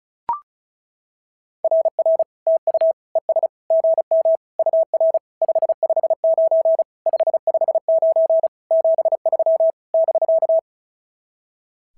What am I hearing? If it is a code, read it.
Morse code, heard as a keyed tone, about 33 wpm, 650 Hz. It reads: RR TU ES GM UR 559 559 73 <BK>
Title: SOTA QSO Practice 35wpm